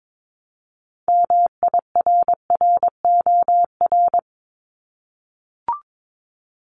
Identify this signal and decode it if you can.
Morse code, heard as a keyed tone, about 22 wpm, 700 Hz. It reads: MIRROR
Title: Multi-letter Phonemes 22wpm